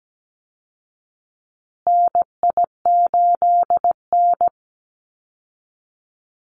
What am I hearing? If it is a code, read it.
Morse code, heard as a keyed tone, about 17 wpm, 700 Hz. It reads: NI8N
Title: Call Signs Repeated 2x - 17wpm